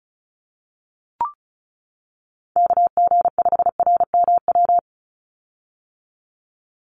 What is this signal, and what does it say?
Morse code, heard as a keyed tone, about 34 wpm, 700 Hz. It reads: KG5RMW
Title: USA Call Signs 35wpm